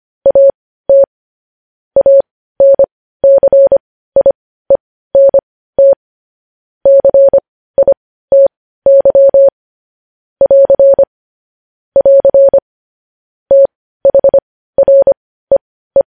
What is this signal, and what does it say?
Morse code, heard as a keyed tone, about 25 wpm, spaced as for 15 wpm, 550 Hz. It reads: AT ANCIENT CITY <AR> <AR> THREE
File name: podcast / News headlines-1x-25x15